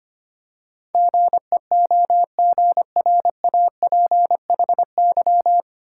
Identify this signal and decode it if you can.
Morse code, heard as a keyed tone, about 25 wpm, 700 Hz. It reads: GEOGRAPHY
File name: Nine Letter Words - 25wpm